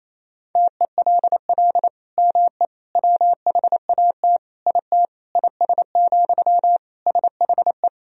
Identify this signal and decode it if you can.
Morse code, heard as a keyed tone, about 28 wpm, 700 Hz. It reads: TELL ME WHAT IT IS, SHE
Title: Sentences from Top 1000 Words - Review Part 1 - 28wpm